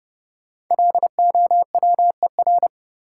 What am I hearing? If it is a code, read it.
Morse code, heard as a keyed tone, about 30 wpm, 700 Hz. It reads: LOWER
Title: Sentences from Top 700 Words - New Word Drill - 30wpm